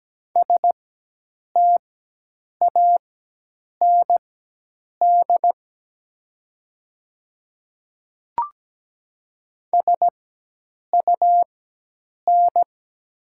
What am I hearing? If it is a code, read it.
Morse code, heard as a keyed tone, about 17 wpm, spaced as for 8 wpm, 700 Hz. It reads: STAND SUN
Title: 4x Character Spacing - Top 300 Words - 17wpm